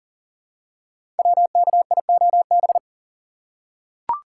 Morse code, heard as a keyed tone, about 38 wpm, 690 Hz. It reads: WKIOB
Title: Code Groups - 5 Letters - 40wpm